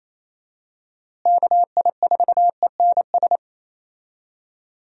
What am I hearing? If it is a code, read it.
Morse code, heard as a keyed tone, about 28 wpm, 700 Hz. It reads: KI4ENS
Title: USA Call Signs - 6 Characters - Repeated 3x - 28wpm